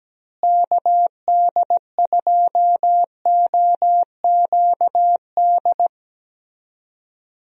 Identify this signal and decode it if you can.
Morse code, heard as a keyed tone, about 17 wpm, 700 Hz. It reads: KD2OQD
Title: USA Call Signs - 6 Characters - Repeated 3x - 17wpm